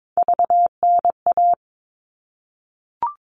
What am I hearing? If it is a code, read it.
Morse code, heard as a keyed tone, about 22 wpm, 700 Hz. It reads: VNA
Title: All Cut Numbers - 3-Digit 22wpm